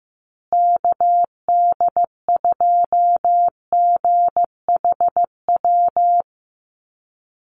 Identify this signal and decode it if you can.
Morse code, heard as a keyed tone, about 15 wpm, 700 Hz. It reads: KD2GHW
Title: USA Call Signs Repeated 3x 15wpm